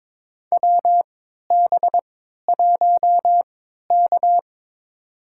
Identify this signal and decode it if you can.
Morse code, heard as a keyed tone, about 22 wpm, spaced as for 12 wpm, 700 Hz. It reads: WB1K
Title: USA Call Signs - 4 Characters - 3x Character Spacing - 22wpm